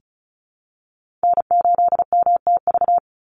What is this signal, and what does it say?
Morse code, heard as a keyed tone, about 34 wpm, 700 Hz. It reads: N8MTV
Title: Call Signs - 35wpm